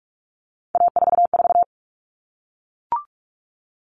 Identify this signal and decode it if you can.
Morse code, heard as a keyed tone, about 44 wpm, 700 Hz. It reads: A44
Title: All Cut Numbers - 3-Digit 45wpm